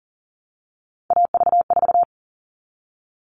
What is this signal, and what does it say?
Morse code, heard as a keyed tone, about 40 wpm, 700 Hz. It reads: AV4
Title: All Cut Numbers - 3-Digit 40wpm